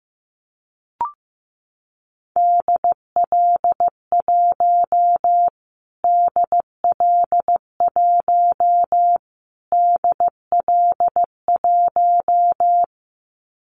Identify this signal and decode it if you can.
Morse code, heard as a keyed tone, about 15 wpm, 700 Hz. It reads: DL1 DL1 DL1
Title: Callsign Prefixes for Top 25 DXCC - Repeated 3x - 15wpm